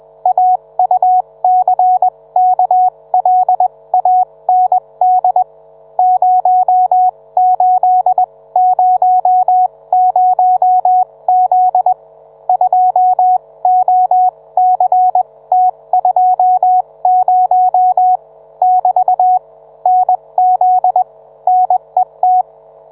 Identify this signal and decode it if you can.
Morse code, heard as a keyed tone, about 21 wpm, spaced as for 18 wpm, 740 Hz. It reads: AUCKLAND 0800Z 2OCT20 <BT> NZ NET